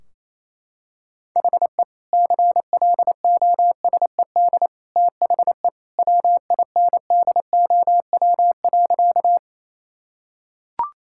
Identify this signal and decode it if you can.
Morse code, heard as a keyed tone, about 28 wpm, 700 Hz. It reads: HE CLOSED THE WINDOW.
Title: Sentences from Top 1600 Words 28wpm